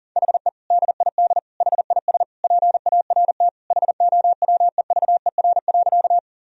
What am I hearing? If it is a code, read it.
Morse code, heard as a keyed tone, about 40 wpm, 700 Hz. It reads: HE DID HIS PART HOWEVER.
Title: Sentences from Top 1000 Words - Review Part 1 - 40wpm